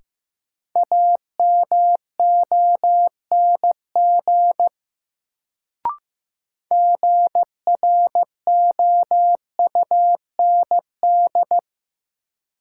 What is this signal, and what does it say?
Morse code, heard as a keyed tone, about 15 wpm, 700 Hz. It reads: AMONG GROUND
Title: Sentences from Top 500 Words - New Word Drill - 15wpm